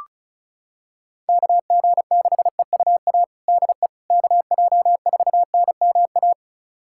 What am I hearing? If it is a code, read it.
Morse code, heard as a keyed tone, about 34 wpm, 700 Hz. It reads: KG6EUA DE KJ4NMA
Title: USA Callsign DE Callsign - 6 Characters - 35wpm